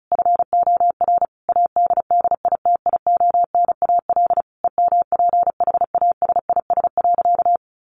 Morse code, heard as a keyed tone, about 34 wpm, 700 Hz. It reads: FOR ADDITIONAL EMPHASIS.
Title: Sentences from Top 1400 Words 35wpm